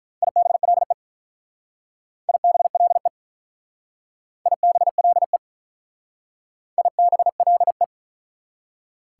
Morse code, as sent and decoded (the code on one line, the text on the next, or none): .. -... .-.. / .. -... .-.. / .. -... .-.. / .. -... .-.. .
IBLE IBLE IBLE IBLE